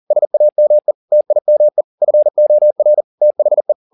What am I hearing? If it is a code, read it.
Morse code, heard as a keyed tone, about 40 wpm, 600 Hz. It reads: SAME TIME FOR THE